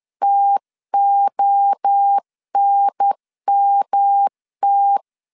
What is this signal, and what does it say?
Morse code, heard as a keyed tone, about 10 wpm, 800 Hz. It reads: TONMT
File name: 10wpm